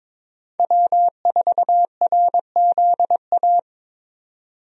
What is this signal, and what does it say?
Morse code, heard as a keyed tone, about 22 wpm, 700 Hz. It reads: W4RZA
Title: USA Call Signs - 5 Characters - 22wpm